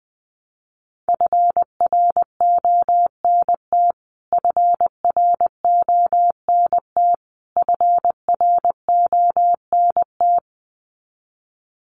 Morse code, heard as a keyed tone, about 20 wpm, 700 Hz. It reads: FRONT FRONT FRONT
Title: Top 500 Words Repeated 20wpm